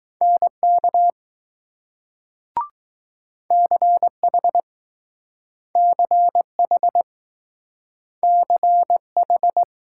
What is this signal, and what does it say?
Morse code, heard as a keyed tone, about 20 wpm, 700 Hz. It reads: NK CH CH CH